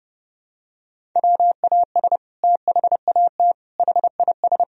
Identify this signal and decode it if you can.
Morse code, heard as a keyed tone, about 30 wpm, 700 Hz. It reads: WAS THAT HIS
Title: Sentences from Top 1000 Words - Review Part 1 - 30wpm